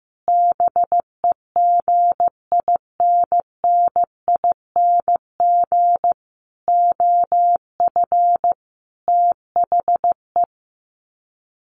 Morse code, heard as a keyed tone, about 15 wpm, 700 Hz. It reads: BEGINNING OF THE
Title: Sets of 5 Words Encore2 15wpm